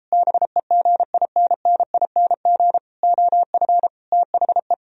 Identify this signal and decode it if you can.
Morse code, heard as a keyed tone, about 33 wpm, 700 Hz. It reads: BEGINNING OF THE